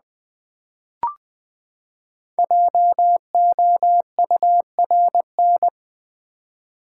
Morse code, heard as a keyed tone, about 20 wpm, 700 Hz. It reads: JOURN
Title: Build A Word - 5 Letters At A Time - 10-Letter Words - 20wpm